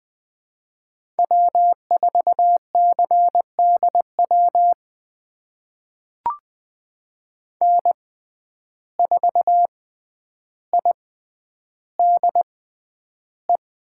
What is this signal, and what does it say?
Morse code, heard as a keyed tone, about 20 wpm, spaced as for 10 wpm, 700 Hz. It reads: W4CDW N4IDE
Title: USA Call Signs - 5 Characters - 6x Character Spacing - 20wpm